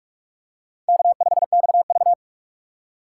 Morse code, heard as a keyed tone, about 44 wpm, 690 Hz. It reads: K5XV
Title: USA Call Signs - 3x Character Spacing - 45wpm